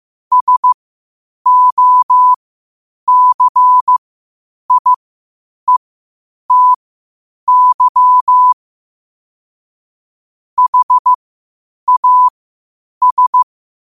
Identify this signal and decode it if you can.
Morse code, heard as a keyed tone, about 15 wpm, spaced as for 8 wpm, 1000 Hz. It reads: SOCIETY HAS
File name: qotdfiles / 8wpm